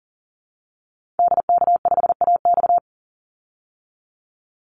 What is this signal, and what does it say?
Morse code, heard as a keyed tone, about 40 wpm, 700 Hz. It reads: DK5AX
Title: Call Signs Repeated 2x - 40wpm